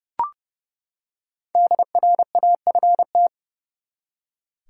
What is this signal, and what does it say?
Morse code, heard as a keyed tone, about 31 wpm, 700 Hz. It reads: DRAFT
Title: Five Letter Words - 30wpm